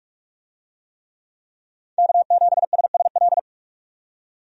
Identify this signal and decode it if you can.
Morse code, heard as a keyed tone, about 42 wpm, 700 Hz. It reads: K7SSL
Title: USA Call Signs - 5 Characters - 45wpm